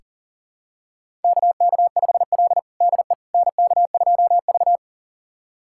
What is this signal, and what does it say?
Morse code, heard as a keyed tone, about 37 wpm, 700 Hz. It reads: KK5L DE NK2V
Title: USA Callsign DE Callsign - 4 Characters - 40wpm